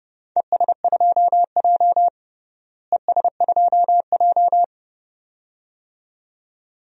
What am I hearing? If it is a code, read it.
Morse code, heard as a keyed tone, about 30 wpm, 700 Hz. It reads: ES2J ES2J
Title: DXCC-100 Call Signs - 4 Characters - Repeated 2x - 30wpm